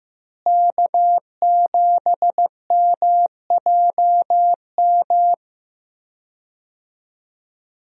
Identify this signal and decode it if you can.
Morse code, heard as a keyed tone, about 15 wpm, 700 Hz. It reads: K7MJM
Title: USA Call Signs - 5 Characters - Repeated 3x - 15wpm